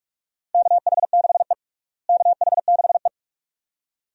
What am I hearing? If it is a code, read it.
Morse code, heard as a keyed tone, about 42 wpm, 700 Hz. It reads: KH6E KH6E
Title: DXCC-100 Call Signs - 4 Characters - Repeated 2x - 45wpm